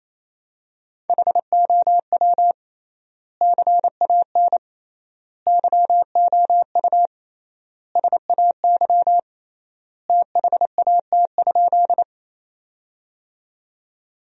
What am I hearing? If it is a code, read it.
Morse code, heard as a keyed tone, about 28 wpm, 700 Hz. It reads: HOW CAN YOU SAY THAT?
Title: 3x Word-Spacing - Sentences from Top 200 Words - 28wpm